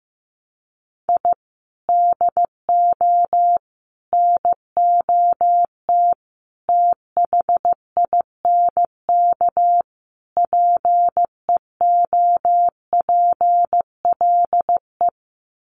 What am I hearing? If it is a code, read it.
Morse code, heard as a keyed tone, about 15 wpm, 700 Hz. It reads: I DO NOT THINK PEOPLE
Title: Sets of 5 Words Encore2 15wpm